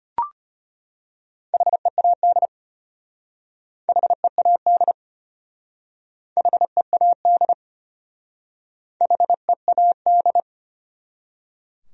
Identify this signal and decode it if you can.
Morse code, heard as a keyed tone, about 31 wpm, 700 Hz. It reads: HEAD HEAD HEAD HEAD
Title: Speed-Racing Top 300 Words 25wpm